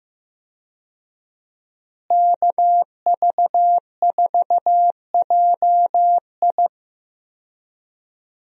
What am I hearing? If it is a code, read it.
Morse code, heard as a keyed tone, about 15 wpm, 700 Hz. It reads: KV4JI
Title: USA Call Signs - 5 Characters - Repeated 3x - 15wpm